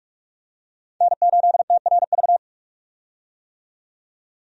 Morse code, heard as a keyed tone, about 44 wpm, 700 Hz. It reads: N8TRV
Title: USA Call Signs - 5 Characters - 45wpm